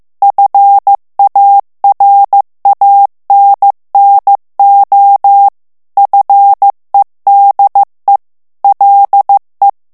Morse code, heard as a keyed tone, about 15 wpm, 800 Hz. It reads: FARANNO FEDE LE
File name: morse mp3 / ITALIEN texte clair 2 vitesse 14 mots minute